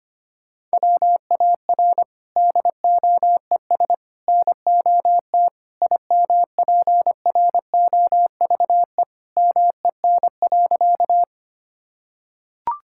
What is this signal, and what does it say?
Morse code, heard as a keyed tone, about 25 wpm, 700 Hz. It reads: WAR DOES NOT IMPROVE MEN.
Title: Sentences from Top 1000 Words - Review Part 5 - 25wpm